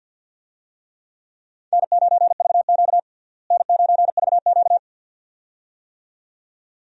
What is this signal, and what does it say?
Morse code, heard as a keyed tone, about 52 wpm, 700 Hz. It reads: N9VQ N9VQ
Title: USA Call Signs - 4 Characters - Repeated 2x - 50wpm